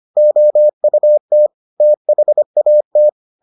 Morse code, heard as a keyed tone, about 25 wpm, 600 Hz. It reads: OUT THAT